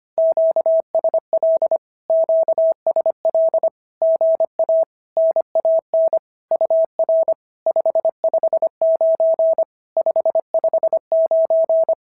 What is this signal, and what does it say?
Morse code, heard as a keyed tone, about 25 wpm, 650 Hz. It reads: QSL QSL GA NAN UR 559 559
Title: SOTA QSO Practice 25wpm